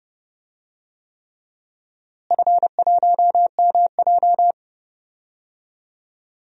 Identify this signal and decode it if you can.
Morse code, heard as a keyed tone, about 30 wpm, 700 Hz. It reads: F1MJ
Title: DXCC-100 Call Signs - 4 Characters - Repeated 3x - 30wpm